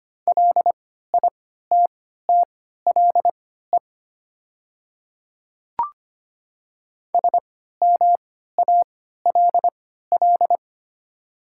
Morse code, heard as a keyed tone, about 25 wpm, spaced as for 14 wpm, 700 Hz. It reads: LITTLE SMALL
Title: 3x Character Spacing - Top 200 Words - 25wpm